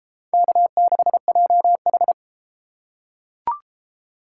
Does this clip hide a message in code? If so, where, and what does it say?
Yes: Morse code from 0.3 to 2.1 s, heard as a keyed tone, about 32 wpm, 700 Hz. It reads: K6JH